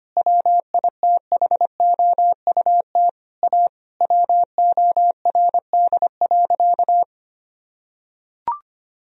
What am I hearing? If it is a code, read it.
Morse code, heard as a keyed tone, about 25 wpm, 700 Hz. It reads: WITHOUT A WORD.
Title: Sentences from Top 1000 Words - Review Part 3 - 25wpm